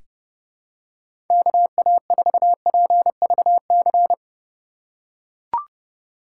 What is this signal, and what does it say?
Morse code, heard as a keyed tone, about 31 wpm, 700 Hz. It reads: KA4PVC
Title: USA Call Signs - 6 Characters - 3x Character Spacing - 30wpm